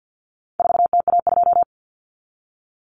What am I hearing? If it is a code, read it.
Morse code, heard as a keyed tone, about 50 wpm, 700 Hz. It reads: 4TA2
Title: Common Cut Numbers - 4-Digit 50wpm